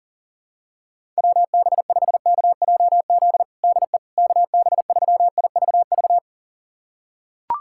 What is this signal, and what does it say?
Morse code, heard as a keyed tone, about 40 wpm, 700 Hz. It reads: WB5KJZ DE KB3IVV
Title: USA Callsign DE Callsign - 6 Characters - 40wpm